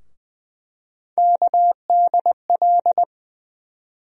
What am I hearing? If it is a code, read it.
Morse code, heard as a keyed tone, about 20 wpm, 700 Hz. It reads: KDL
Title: USA Call Sign Postfixes - 3 Letters 20wpm